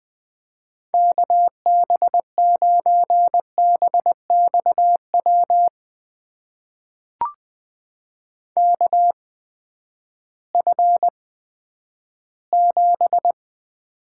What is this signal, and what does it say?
Morse code, heard as a keyed tone, about 20 wpm, spaced as for 11 wpm, 700 Hz. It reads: KB9BXW KF7
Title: USA Call Signs - 6 Characters - 8x Character Spacing - 20wpm